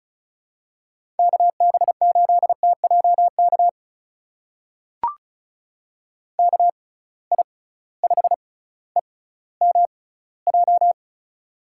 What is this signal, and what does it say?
Morse code, heard as a keyed tone, about 36 wpm, spaced as for 18 wpm, 700 Hz. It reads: KB8TJK KI5EMJ
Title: USA Call Signs - 6 Characters - 6x Character Spacing - 35wpm